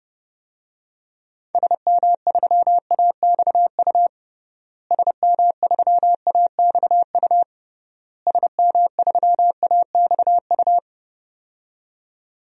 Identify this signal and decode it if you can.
Morse code, heard as a keyed tone, about 30 wpm, 700 Hz. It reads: SM3AXU SM3AXU SM3AXU
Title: DXCC-100 Call Signs - 6 Characters - Repeated 3x - 30wpm